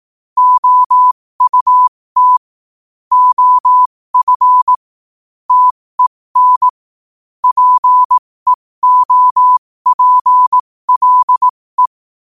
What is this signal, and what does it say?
Morse code, heard as a keyed tone, about 18 wpm, spaced as for 15 wpm, 1000 Hz. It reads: OUT OF TEN PEOPLE
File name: qotdfiles / 15wpm